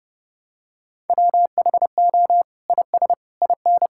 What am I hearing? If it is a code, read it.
Morse code, heard as a keyed tone, about 30 wpm, 700 Hz. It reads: WHO IS IN
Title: Sentences from Top 1000 Words - Review Part 6 - 30wpm